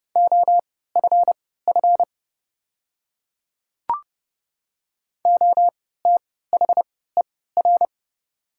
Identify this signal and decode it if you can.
Morse code, heard as a keyed tone, about 30 wpm, spaced as for 17 wpm, 690 Hz. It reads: OFF OTHER
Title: 3x Character Spacing - Top 200 Words - 30wpm